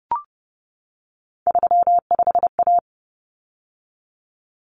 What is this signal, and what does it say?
Morse code, heard as a keyed tone, about 30 wpm, 700 Hz. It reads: 35A
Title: Common Cut Numbers - 3-Digit 30wpm